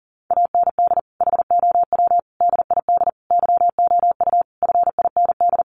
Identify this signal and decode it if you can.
Morse code, heard as a keyed tone, about 40 wpm, 700 Hz. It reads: AND HOW DID YOU FIND
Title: Sentences from Top 100 Words 40wpm